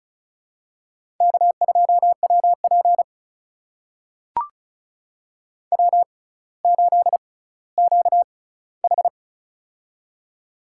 Morse code, heard as a keyed tone, about 34 wpm, spaced as for 18 wpm, 700 Hz. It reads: K2WP W8QH
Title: USA Call Signs - 4 Characters - 6x Character Spacing - 35wpm